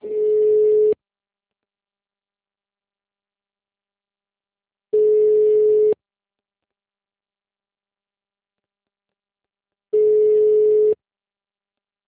Ringback tone (Europe, 425 Hz).
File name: out- -2025- - - .37